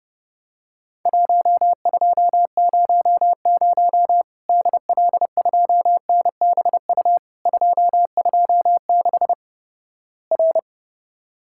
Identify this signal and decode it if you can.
Morse code, heard as a keyed tone, about 31 wpm, 700 Hz. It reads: 1200 DL2NBU 226 R
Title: Worked All Europe DX Contest - QTC Exchange - 30wpm